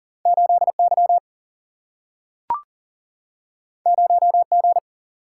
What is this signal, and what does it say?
Morse code, heard as a keyed tone, about 39 wpm, 700 Hz. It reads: 8Y 0G
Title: Single-Letter Single-Number 40wpm